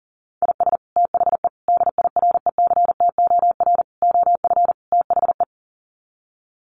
Morse code, heard as a keyed tone, about 40 wpm, 700 Hz. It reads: IS THE DIRECTOR OF THE
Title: Sets of 5 Words Encore2 40wpm